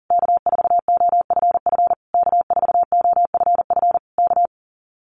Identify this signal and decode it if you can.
Morse code, heard as a keyed tone, about 40 wpm, 700 Hz. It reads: K4OFF K4OFF K
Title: Common QSO Phrases Encore 40wpm